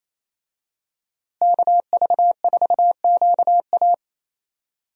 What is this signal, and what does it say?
Morse code, heard as a keyed tone, about 28 wpm, 700 Hz. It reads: KV4QA
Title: USA Call Signs - 5 Characters - Repeated 3x - 28wpm